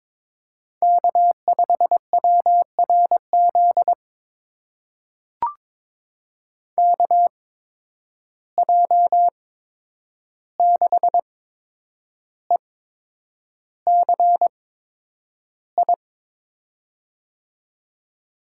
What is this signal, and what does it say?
Morse code, heard as a keyed tone, about 22 wpm, spaced as for 9 wpm, 700 Hz. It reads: K5WRZ KJ6ECI
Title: USA Call Signs - 8x Character Spacing - 22wpm